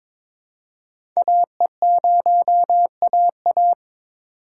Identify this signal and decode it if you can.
Morse code, heard as a keyed tone, about 22 wpm, 700 Hz. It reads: AE0AA
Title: USA Call Signs - 5 Characters - 3x Character Spacing - 22wpm